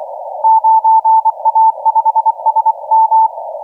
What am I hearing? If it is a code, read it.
Morse code, heard as a keyed tone, about 24 wpm, 890 Hz. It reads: 9A5SM
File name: HH5 KC0W 1353UTC 15m